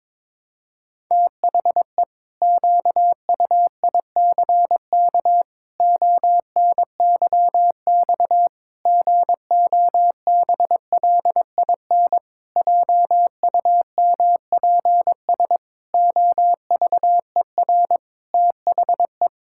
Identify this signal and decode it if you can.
Morse code, heard as a keyed tone, about 22 wpm, 700 Hz. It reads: THE QUICK ONYX GOBLIN JUMPS OVER THE
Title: Pangram Challenge v2 - 22wpm